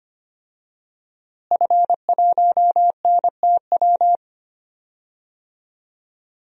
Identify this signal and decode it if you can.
Morse code, heard as a keyed tone, about 25 wpm, 700 Hz. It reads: F1NTW
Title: DXCC-100 Call Signs - 5 Characters - Repeated 2x - 25wpm